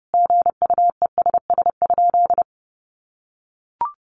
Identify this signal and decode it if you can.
Morse code, heard as a keyed tone, about 30 wpm, 700 Hz. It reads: GUESS?
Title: Sentences from Top 1500 Words 30wpm